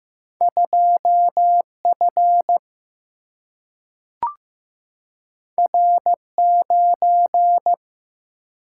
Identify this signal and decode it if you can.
Morse code, heard as a keyed tone, about 15 wpm, 700 Hz. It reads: 2 F R 9
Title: Single-Letter Single-Number 15wpm